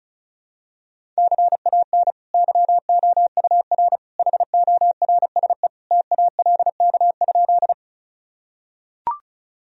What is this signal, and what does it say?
Morse code, heard as a keyed tone, about 34 wpm, 700 Hz. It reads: CAN YOUR HORSE TALK?
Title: Sentences from Top 1000 Words - Review Part 1 - 35wpm